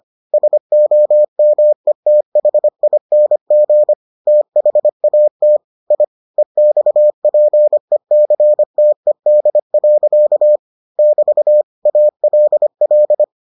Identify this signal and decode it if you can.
Morse code, heard as a keyed tone, about 25 wpm, 600 Hz. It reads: SOMETHING THAT I EXPECTED. <BT> ALL